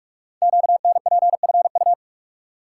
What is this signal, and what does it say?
Morse code, heard as a keyed tone, about 44 wpm, 700 Hz. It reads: QNPFU
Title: Code Groups - 5 Letters - 45wpm